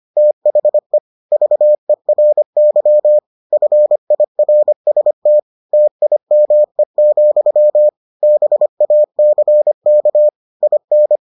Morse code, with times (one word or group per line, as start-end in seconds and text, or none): 0.2-1.0 THE
1.3-3.2 VERY
3.5-5.4 FIRST
5.7-7.9 TIME,
8.2-10.3 BACK
10.6-11.2 IN